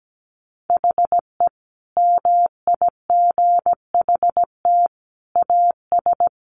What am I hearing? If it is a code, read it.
Morse code, heard as a keyed tone, about 17 wpm, 700 Hz. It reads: HE MIGHT AS
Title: Sentences from Top 1300 Words 17wpm